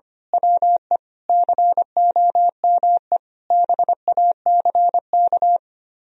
Morse code, heard as a keyed tone, about 24 wpm, 700 Hz. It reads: WE COME BACK
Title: Speed-Racing - Sets of 5 Words 25wpm